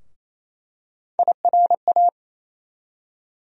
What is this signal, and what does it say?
Morse code, heard as a keyed tone, about 27 wpm, 700 Hz. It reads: IRA
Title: USA Call Sign Postfixes - 3 Letters 28wpm